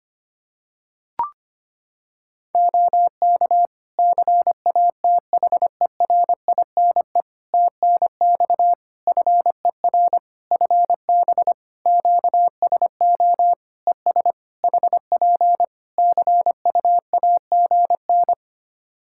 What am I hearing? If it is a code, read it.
Morse code, heard as a keyed tone, about 25 wpm, 700 Hz. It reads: OK CATHERINE TNX FER FB QSO ES HP CUAGN
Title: Practice QSO - Exchange 4 - Ending - 25wpm